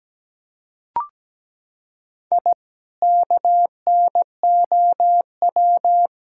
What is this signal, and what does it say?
Morse code, heard as a keyed tone, about 17 wpm, 700 Hz. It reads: I KNOW
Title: Sentences from Top 1000 Words - Review Part 6 - 17wpm